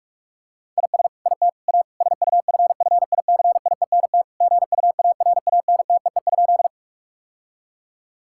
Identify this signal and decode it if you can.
Morse code, heard as a keyed tone, about 46 wpm, 700 Hz. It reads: IS IT A SUFFICIENT GUARANTEE?
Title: Sentences from Top 1600 Words 45wpm